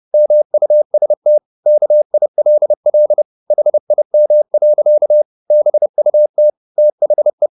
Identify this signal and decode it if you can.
Morse code, heard as a keyed tone, about 30 wpm, 600 Hz. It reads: MUST KILL HIM. BUT THE